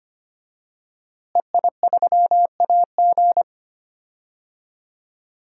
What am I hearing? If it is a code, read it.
Morse code, heard as a keyed tone, about 25 wpm, 700 Hz. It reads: EI3AG
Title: DXCC-100 Call Signs - 5 Characters - Repeated 3x - 25wpm